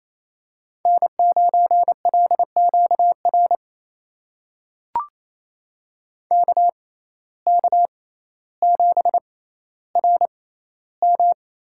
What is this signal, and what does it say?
Morse code, heard as a keyed tone, about 28 wpm, spaced as for 14 wpm, 700 Hz. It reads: N9LQR KK7RM
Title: USA Call Signs - 5 Characters - 6x Character Spacing - 28wpm